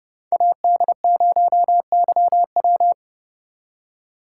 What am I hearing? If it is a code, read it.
Morse code, heard as a keyed tone, about 33 wpm, 700 Hz. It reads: AD0YW